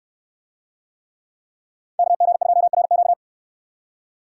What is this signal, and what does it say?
Morse code, heard as a keyed tone, about 66 wpm, 700 Hz. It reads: KG2RQ